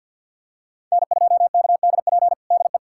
Morse code, heard as a keyed tone, about 53 wpm, 700 Hz. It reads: N2KDP DE
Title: USA Callsign DE Callsign - 5 Characters - 50wpm